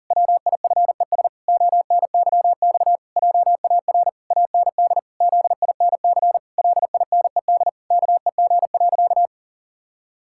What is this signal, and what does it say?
Morse code, heard as a keyed tone, about 37 wpm, 700 Hz. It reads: WIFES ONYX JAR AND ZINC LINED KEG.
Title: Pangram Challenge v2 - 40wpm